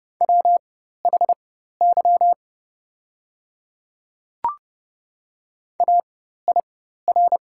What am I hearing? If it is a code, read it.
Morse code, heard as a keyed tone, about 30 wpm, spaced as for 14 wpm, 690 Hz. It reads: WHY AIR
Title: 4x Character Spacing - Top 200 Words - 30wpm